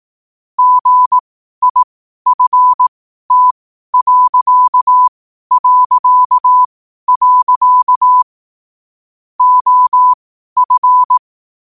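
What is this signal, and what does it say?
Morse code, heard as a keyed tone, about 18 wpm, spaced as for 12 wpm, 1000 Hz. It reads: GIFT... OF
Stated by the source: Morse at 12 wpm.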